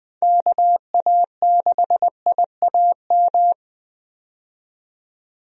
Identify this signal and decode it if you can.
Morse code, heard as a keyed tone, about 20 wpm, 700 Hz. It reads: KA6IAM
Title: USA Call Signs - 6 Characters - 20wpm